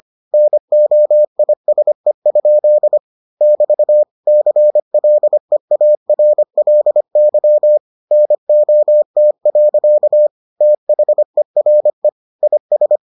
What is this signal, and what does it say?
Morse code, heard as a keyed tone, about 25 wpm, 600 Hz. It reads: NOISE? <BT> CLEARLY NOT. THERE IS